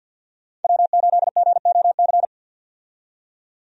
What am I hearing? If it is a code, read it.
Morse code, heard as a keyed tone, about 51 wpm, 700 Hz. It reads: W8GOC
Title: USA Call Signs - 5 Characters - 50wpm